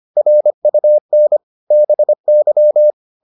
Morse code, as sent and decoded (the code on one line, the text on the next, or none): .-. ..- -. / -... -.--
RUN BY